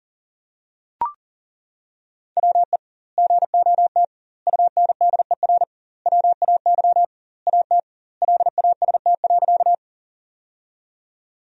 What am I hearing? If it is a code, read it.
Morse code, heard as a keyed tone, about 40 wpm, 700 Hz. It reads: WE GOT UNDER WAY AT LAST.
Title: 2x Word-Spacing - Sentences from Top 300 Words - 40wpm